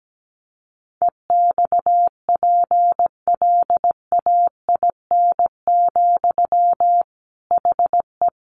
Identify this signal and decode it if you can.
Morse code, heard as a keyed tone, about 17 wpm, 700 Hz. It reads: EXPLAIN, HE
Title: Sentences from Top 1100 Words 17wpm